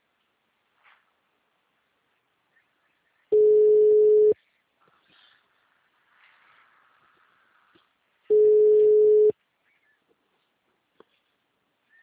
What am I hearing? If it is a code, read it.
Ringback tone (Europe, 425 Hz).
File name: out- -2003- - - .9500